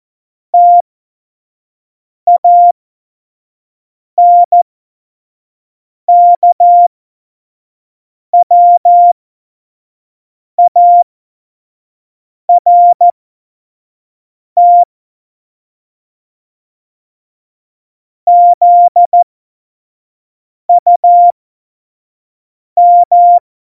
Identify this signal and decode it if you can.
Morse code, heard as a keyed tone, about 14 wpm, spaced as for 5 wpm, 700 Hz. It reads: TANKWART ZUM
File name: Morsekurs / Witze / Witz 12. Februar 2019, 25 F70